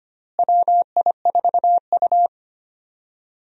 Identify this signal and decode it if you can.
Morse code, heard as a keyed tone, about 25 wpm, 700 Hz. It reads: WI4U
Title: USA Call Signs - 4 Characters - Repeated 3x - 25wpm